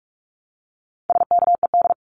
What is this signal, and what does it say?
Morse code, heard as a keyed tone, about 44 wpm, 700 Hz. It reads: SKED
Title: Abbreviations Repeated 3x - 45wpm